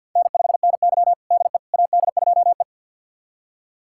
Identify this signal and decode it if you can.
Morse code, heard as a keyed tone, about 46 wpm, 700 Hz. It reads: N5NY DE AD2E
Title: USA Callsign DE Callsign - 4 Characters - 50wpm